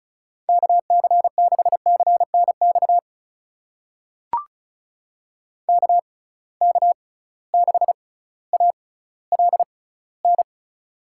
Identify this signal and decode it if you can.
Morse code, heard as a keyed tone, about 36 wpm, spaced as for 18 wpm, 700 Hz. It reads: KC6CNX KK6ALN
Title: USA Call Signs - 6 Characters - 6x Character Spacing - 35wpm